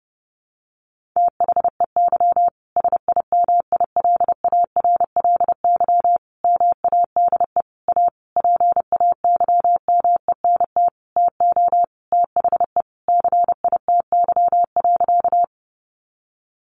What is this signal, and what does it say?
Morse code, heard as a keyed tone, about 30 wpm, 700 Hz. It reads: THEY SIMILARLY MADE A PAYMENT TO THE CITY.
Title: Sentences from Top 1400 Words 30wpm